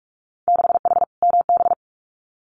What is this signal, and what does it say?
Morse code, heard as a keyed tone, about 46 wpm, 700 Hz. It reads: 6H MB
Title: Winter Field Day Exchanges Encore 45wpm